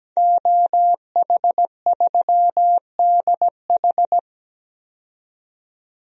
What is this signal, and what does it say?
Morse code, heard as a keyed tone, about 17 wpm, 700 Hz. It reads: OH3DH
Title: DXCC-100 Call Signs - 5 Characters - 17wpm